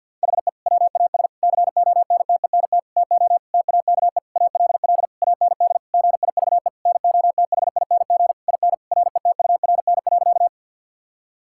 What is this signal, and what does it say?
Morse code, heard as a keyed tone, about 49 wpm, 700 Hz. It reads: HE WAS CONTENT TO TAKE ALL AND GIVE NOTHING IN RETURN.
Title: Sentences from Top 1000 Words - Review Part 6 - 50wpm